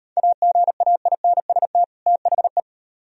Morse code, heard as a keyed tone, about 38 wpm, 700 Hz. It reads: AGAINST THE